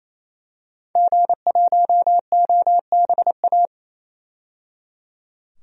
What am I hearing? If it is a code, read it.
Morse code, heard as a keyed tone, about 28 wpm, 700 Hz. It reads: G1OBA
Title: DXCC-100 Call Signs - 5 Characters - 28wpm